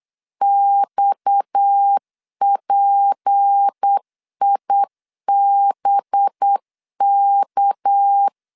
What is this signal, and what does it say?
Morse code, heard as a keyed tone, about 8 wpm, 800 Hz. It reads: XPIBK
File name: LEZIONI / 6 PKX0 / 6 PKX 8wpm 3041